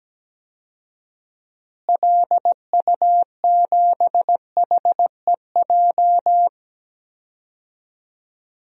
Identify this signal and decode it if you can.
Morse code, heard as a keyed tone, about 17 wpm, 700 Hz. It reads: LU7HEJ
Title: DXCC-100 Call Signs - 6 Characters - Repeated 3x - 17wpm